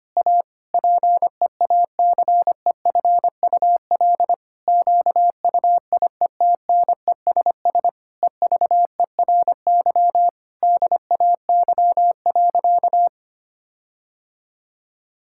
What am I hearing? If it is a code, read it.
Morse code, heard as a keyed tone, about 25 wpm, 700 Hz. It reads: A PEACEFUL QUIETNESS EVERY DAY.
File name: practice / 25wpm / Difficult Sentences with Hints - 25wpm